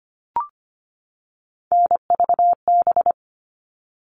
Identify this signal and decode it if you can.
Morse code, heard as a keyed tone, about 25 wpm, 700 Hz. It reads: NVB
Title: All Cut Numbers - 3-Digit 25wpm